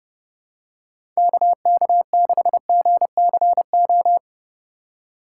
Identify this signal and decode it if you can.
Morse code, heard as a keyed tone, about 30 wpm, 700 Hz. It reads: KK6GCO
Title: USA Call Signs - 4x Character Spacing - 30wpm